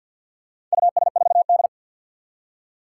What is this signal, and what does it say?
Morse code, heard as a keyed tone, about 54 wpm, 700 Hz. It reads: US4D
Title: Four Challenging Letter-Number Combinations 50wpm